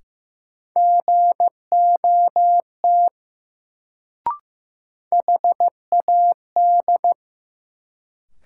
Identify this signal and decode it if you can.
Morse code, heard as a keyed tone, about 16 wpm, 700 Hz. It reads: GOT HAD
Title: Sets of 2 Words - New Word Drill - 15wpm